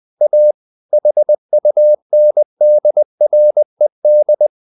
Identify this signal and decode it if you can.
Morse code, heard as a keyed tone, about 20 wpm, 600 Hz. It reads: A HUNDRED
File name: the old man and the sea part 2 20wpm